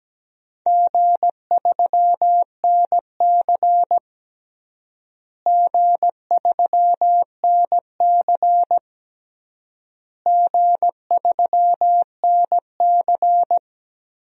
Morse code, heard as a keyed tone, about 17 wpm, 700 Hz. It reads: G3NC G3NC G3NC
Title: DXCC-100 Call Signs - 4 Characters - Repeated 3x - 17wpm